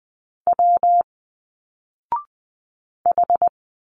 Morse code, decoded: W H